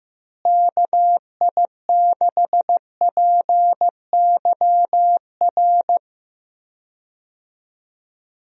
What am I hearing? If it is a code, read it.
Morse code, heard as a keyed tone, about 15 wpm, 700 Hz. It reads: KI6PYR
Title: USA Call Signs - 6 Characters - Repeated 2x - 15wpm